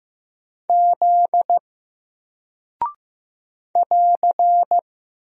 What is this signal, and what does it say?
Morse code, heard as a keyed tone, about 15 wpm, 700 Hz. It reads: Z <AR>
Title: Single Character (Letter:Number:Punctuation:Prosign) 15wpm